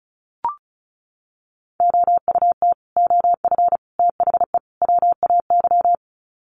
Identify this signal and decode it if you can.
Morse code, heard as a keyed tone, about 34 wpm, 700 Hz. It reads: OUT OF THE WAY
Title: Sets of 4 Words 35wpm